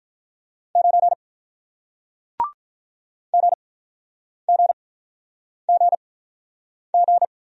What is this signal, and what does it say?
Morse code, heard as a keyed tone, about 48 wpm, spaced as for 9 wpm, 700 Hz. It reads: 9 GGGG